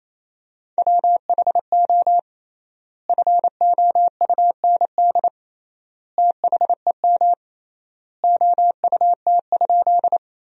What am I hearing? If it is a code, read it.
Morse code, heard as a keyed tone, about 28 wpm, 700 Hz. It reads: WHO FOUND THEM OUT?
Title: 3x Word-Spacing - Sentences from Top 300 Words - 28wpm